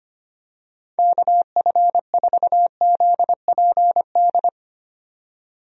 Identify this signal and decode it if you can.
Morse code, heard as a keyed tone, about 25 wpm, 700 Hz. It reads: KF4ZPD
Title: USA Call Signs - 6 Characters - 25wpm